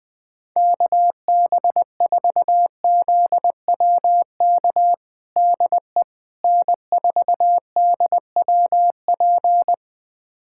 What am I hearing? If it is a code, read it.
Morse code, heard as a keyed tone, about 20 wpm, 700 Hz. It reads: KB4ZWK DE N4DWP
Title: USA Callsign DE Callsign 20wpm